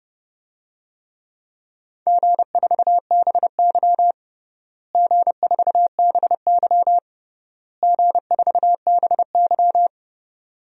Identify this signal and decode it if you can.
Morse code, heard as a keyed tone, about 32 wpm, 700 Hz. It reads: G4BY G4BY G4BY
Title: DXCC-100 Call Signs - 4 Characters - Repeated 3x - 30wpm